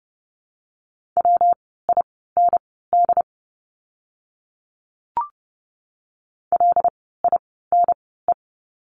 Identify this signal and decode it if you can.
Morse code, heard as a keyed tone, about 30 wpm, spaced as for 17 wpm, 700 Hz. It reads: WIND LINE
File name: course / 30:10wpm / 115 Lesson 19f - Letter w - Words 30wpm